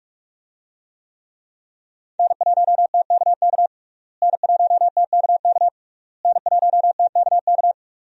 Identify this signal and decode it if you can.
Morse code, heard as a keyed tone, about 47 wpm, 700 Hz. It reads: N1TKK N1TKK N1TKK
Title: USA Call Signs - 5 Characters - Repeated 3x - 45wpm